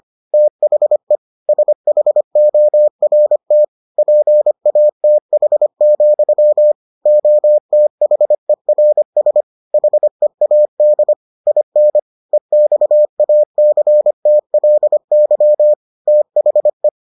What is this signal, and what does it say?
Morse code, heard as a keyed tone, about 25 wpm, 600 Hz. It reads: THE SHORT PATH, OTHERS HEAD IN EXACTLY THE